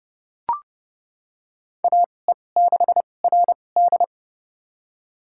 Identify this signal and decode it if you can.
Morse code, heard as a keyed tone, about 30 wpm, spaced as for 22 wpm, 700 Hz. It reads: AE6RD
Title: USA Call Signs - 5 Characters - 2x Character Spacing - 30wpm